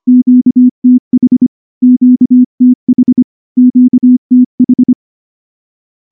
Morse code, heard as a keyed tone, about 24 wpm, 250 Hz. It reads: QTH QTH QTH
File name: morse / morse-training-25WPM